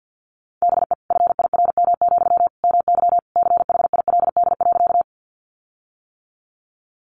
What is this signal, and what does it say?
Morse code, heard as a keyed tone, about 50 wpm, 700 Hz. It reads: BE FIRM, MY CHILD.
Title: Sentences from Top 1400 Words 50wpm